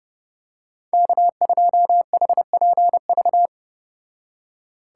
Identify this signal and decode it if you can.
Morse code, heard as a keyed tone, about 29 wpm, 700 Hz. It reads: K2HPV